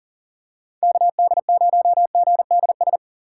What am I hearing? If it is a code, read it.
Morse code, heard as a keyed tone, about 40 wpm, 700 Hz. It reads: KD0GDS
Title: USA Call Signs - 6 Characters - 6x Character Spacing - 40wpm